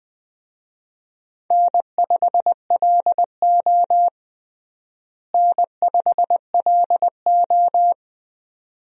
Morse code, heard as a keyed tone, about 20 wpm, 700 Hz. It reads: N5LO N5LO
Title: USA Call Signs - 4 Characters - Repeated 2x - 20wpm